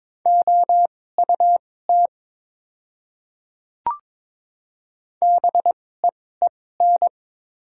Morse code, heard as a keyed tone, about 23 wpm, spaced as for 16 wpm, 690 Hz. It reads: OUT BEEN
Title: 2x Character Spacing - Top 300 Words - 22wpm